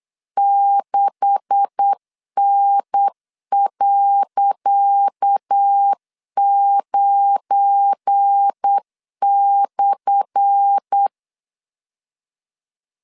Morse code, heard as a keyed tone, about 8 wpm, 800 Hz. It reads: 6N.9/
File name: LEZIONI / 10 SIGNS / 10D COMA 8wpm 3749